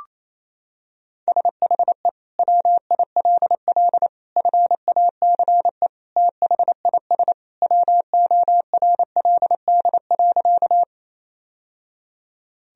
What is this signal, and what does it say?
Morse code, heard as a keyed tone, about 28 wpm, 700 Hz. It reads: SHE WILL FACE THIS WORLD.
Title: Sentences from Top 1000 Words - Review Part 2 - 28wpm